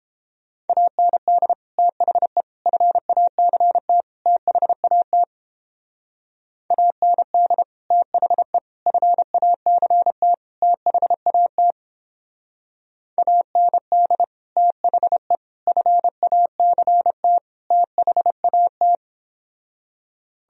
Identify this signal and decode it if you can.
Morse code, heard as a keyed tone, about 31 wpm, 700 Hz. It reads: AND THE FACT THAT AND THE FACT THAT AND THE FACT THAT